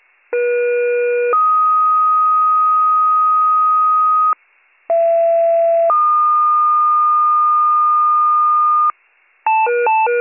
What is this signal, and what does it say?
Two-tone page, heard as a one-second tone, then a three-second tone, about 480 then 1250 Hz, about 670 then 1220 Hz. It is heard from 0.3 to 8.9 s.